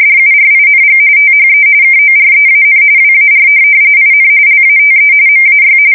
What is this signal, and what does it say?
RTTY, about 45.45 baud. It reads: DURING THE KOREAN WAR IN LATE MAY
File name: fsk170 / battleshipnj